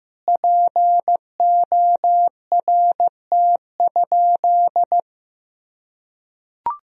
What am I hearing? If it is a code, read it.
Morse code, heard as a keyed tone, about 15 wpm, 700 Hz. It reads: PORT?
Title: Sentences from Top 1000 Words - Review Part 1 - 15wpm